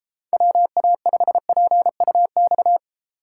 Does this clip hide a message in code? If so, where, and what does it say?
Yes: Morse code from 0.3 to 2.8 s, heard as a keyed tone, about 32 wpm, 700 Hz. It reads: WA5PUX